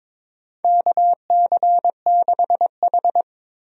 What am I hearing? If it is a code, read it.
Morse code, heard as a keyed tone, about 22 wpm, 700 Hz. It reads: KC6H
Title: USA Call Signs - 4 Characters - Repeated 2x - 22wpm